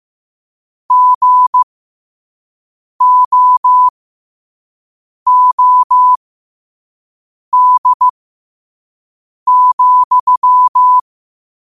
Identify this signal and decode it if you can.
Morse code, heard as a keyed tone, about 15 wpm, spaced as for 5 wpm, 1000 Hz. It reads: GOOD,
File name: 5wpm